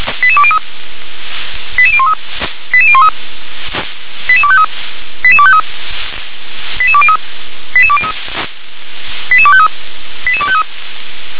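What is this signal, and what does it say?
Five-tone selective call (ZVEI), heard as quick runs of five tones, tones of about 70 ms. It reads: 88203 88013 88013 88253 88253 88203 88203 88253 88253
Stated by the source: ZVEI five-tone call received by SDR.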